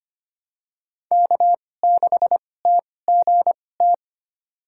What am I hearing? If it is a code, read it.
Morse code, heard as a keyed tone, about 25 wpm, spaced as for 18 wpm, 700 Hz. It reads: K6TGT
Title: USA Call Signs - 5 Characters - 2x Character Spacing - 25wpm